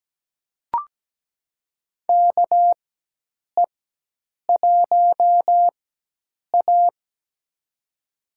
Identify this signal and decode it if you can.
Morse code, heard as a keyed tone, about 17 wpm, spaced as for 8 wpm, 700 Hz. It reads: KE1A
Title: USA Call Signs - 4 Characters - 4x Character Spacing - 17wpm